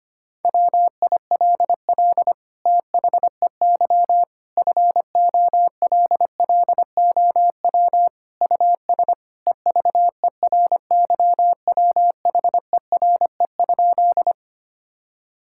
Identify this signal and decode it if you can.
Morse code, heard as a keyed tone, about 25 wpm, 700 Hz. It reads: WILL THEY FOLLOW US EVERYWHERE?
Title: Sentences from Top 1600 Words 25wpm